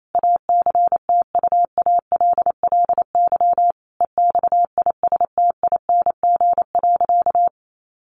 Morse code, heard as a keyed tone, about 28 wpm, 700 Hz. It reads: ACTUALLY EXISTING.
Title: Sentences from Top 1500 Words 28wpm